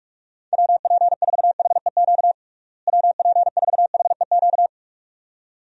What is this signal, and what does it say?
Morse code, heard as a keyed tone, about 46 wpm, 700 Hz. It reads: WP4HEQ WP4HEQ
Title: USA Call Signs - 6 Characters - Repeated 2x - 45wpm